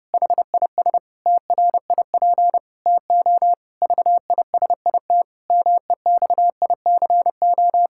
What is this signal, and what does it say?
Morse code, heard as a keyed tone, about 30 wpm, 700 Hz. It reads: HIS TRIP TO VISIT MEXICO
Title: Pangram Challenge v2 - 30wpm